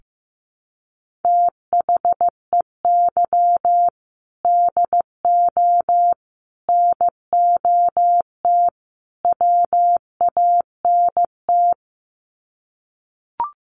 Morse code, heard as a keyed tone, about 15 wpm, 700 Hz. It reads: THEY DO NOT WANT
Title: Sets of 4 Words 15wpm